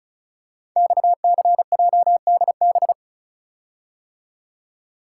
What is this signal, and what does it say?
Morse code, heard as a keyed tone, about 36 wpm, 700 Hz. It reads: XCJDB
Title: Code Groups - 5 Letters - 35wpm